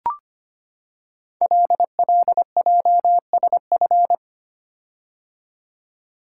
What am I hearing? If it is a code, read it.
Morse code, heard as a keyed tone, about 25 wpm, 700 Hz. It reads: LLJSF
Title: Code Groups - 5 Letters - 25wpm